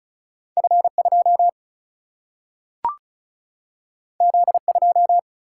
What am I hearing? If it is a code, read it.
Morse code, heard as a keyed tone, about 36 wpm, 700 Hz. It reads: F 2 Z 2
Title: Single-Letter Single-Number 35wpm